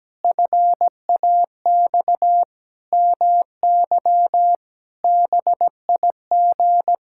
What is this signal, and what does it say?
Morse code, heard as a keyed tone, about 17 wpm, 700 Hz. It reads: FAX MY BIG
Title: Pangram Challenge v2 - 17wpm